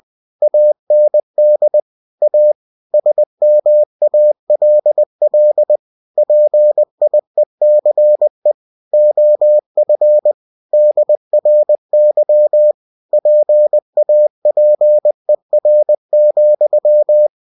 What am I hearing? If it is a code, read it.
Morse code, heard as a keyed tone, about 20 wpm, 600 Hz. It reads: AND A SMALL PIECE OF DRY PAPER,
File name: cw-books / winnie-the-pooh / winnie-20wpm-page-0092